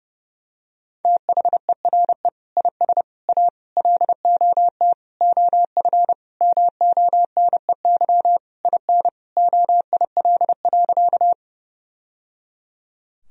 Morse code, heard as a keyed tone, about 30 wpm, 700 Hz. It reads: THERE IS A LOT OF MONEY IN OIL.
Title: Sentences from Top 1000 Words - Review Part 4 - 30wpm